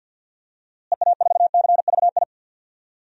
Morse code, heard as a keyed tone, about 50 wpm, 700 Hz. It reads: EA4CVI
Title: DXCC-100 Call Signs - 6 Characters - Repeated 2x - 50wpm